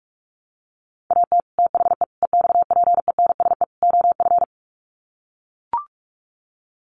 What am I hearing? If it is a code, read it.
Morse code, heard as a keyed tone, about 44 wpm, 700 Hz. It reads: AT THE EXPENSE OF
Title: Sets of 4 Words 45wpm